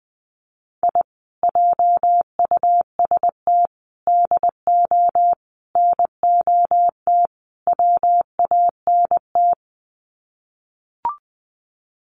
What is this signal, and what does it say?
Morse code, heard as a keyed tone, about 20 wpm, 700 Hz. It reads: I JUST DO NOT WANT
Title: Sets of 5 Words Encore2 20wpm